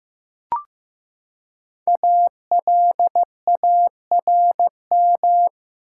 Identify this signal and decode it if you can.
Morse code, heard as a keyed tone, about 15 wpm, 700 Hz. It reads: ALARM
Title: Five Letter Words - 15wpm